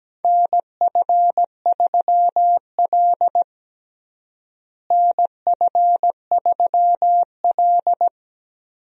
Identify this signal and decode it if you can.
Morse code, heard as a keyed tone, about 17 wpm, 700 Hz. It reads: NF3L NF3L
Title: USA Call Signs - 4 Characters - Repeated 2x - 17wpm